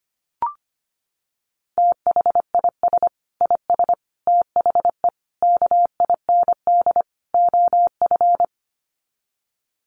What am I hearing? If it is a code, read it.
Morse code, heard as a keyed tone, about 25 wpm, 700 Hz. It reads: THIS IS THE KIND OF
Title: Sets of 5 Words Encore 25wpm